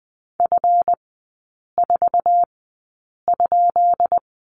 Morse code, heard as a keyed tone, about 20 wpm, spaced as for 8 wpm, 700 Hz. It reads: F4?
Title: 138 Lesson 23e - Letter f - Three Characters 20wpm